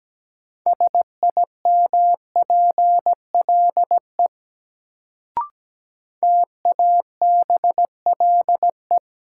Morse code, heard as a keyed tone, about 17 wpm, 700 Hz. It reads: SIMPLE TABLE
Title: Sentences from Top 400 Words - New Word Drill - 17wpm